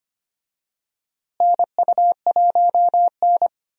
Morse code, heard as a keyed tone, about 25 wpm, 700 Hz. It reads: NU1N
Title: USA Call Signs - 4 Characters - Repeated 3x - 25wpm